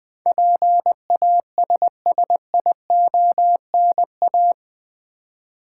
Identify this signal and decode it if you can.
Morse code, heard as a keyed tone, about 20 wpm, 700 Hz. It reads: PASSIONA
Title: Build A Word - 4 Letters At A Time - 10-Letter Words - 20wpm